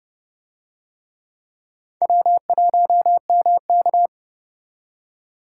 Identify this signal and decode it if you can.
Morse code, heard as a keyed tone, about 30 wpm, 700 Hz. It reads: W1MK